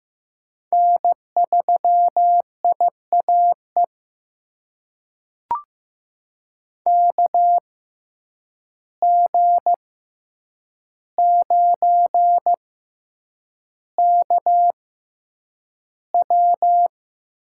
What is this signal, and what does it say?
Morse code, heard as a keyed tone, about 15 wpm, spaced as for 8 wpm, 700 Hz. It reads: N3IAE KG9KW
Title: USA Call Signs - 5 Characters - 6x Character Spacing - 15wpm